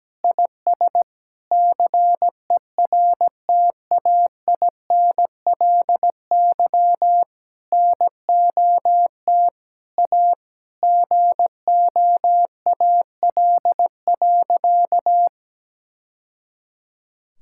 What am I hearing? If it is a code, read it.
Morse code, heard as a keyed tone, about 17 wpm, 700 Hz. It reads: IS CERTAINLY NOT A GOAL.
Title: Sentences from Top 1000 Words - Review Part 5 - 17wpm